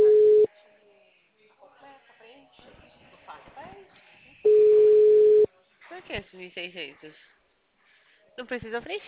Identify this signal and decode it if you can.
Ringback tone (Europe, 425 Hz), with speech.